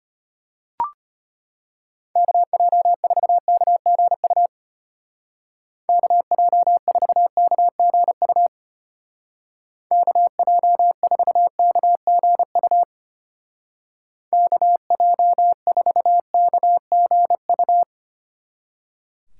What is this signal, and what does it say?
Morse code, heard as a keyed tone, about 33 wpm, 700 Hz. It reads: KJ4KGU KJ4KGU KJ4KGU KJ4KGU